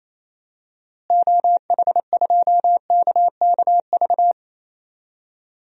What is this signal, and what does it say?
Morse code, heard as a keyed tone, about 28 wpm, 700 Hz. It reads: OH2KKV
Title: DXCC-100 Call Signs - 6 Characters - Repeated 3x - 28wpm